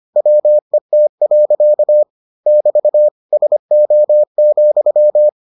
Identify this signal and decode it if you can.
Morse code, heard as a keyed tone, about 25 wpm, 600 Hz. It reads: WET. <BT> SO,